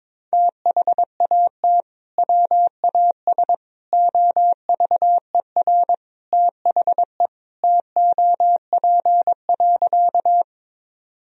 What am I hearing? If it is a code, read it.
Morse code, heard as a keyed tone, about 22 wpm, 700 Hz. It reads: THAT WAS OVER THE TOP.
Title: Sentences from Top 1000 Words - Review Part 2 - 22wpm